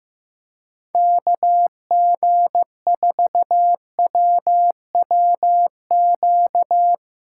Morse code, heard as a keyed tone, about 15 wpm, 700 Hz. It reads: KG4WWQ
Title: USA Call Signs - 3x Character Spacing - 15wpm